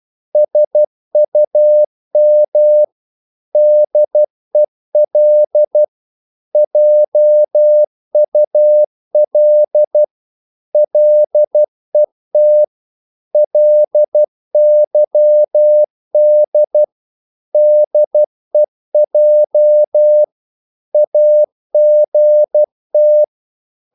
Morse code, heard as a keyed tone, about 12 wpm, 600 Hz. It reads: SUM DEL JUL LET LYD DEJ AGT
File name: Korte ord DK 12wpm